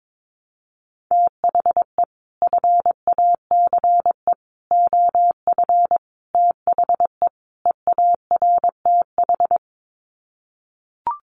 Morse code, heard as a keyed tone, about 22 wpm, 700 Hz. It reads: THE FACE OF THE EARTH
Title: Sets of 5 Words Encore 22wpm